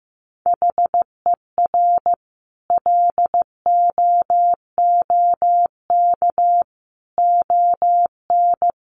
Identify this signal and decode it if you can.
Morse code, heard as a keyed tone, about 14 wpm, 700 Hz. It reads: HER LOOK ON
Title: Sentences from Top 1400 Words 15wpm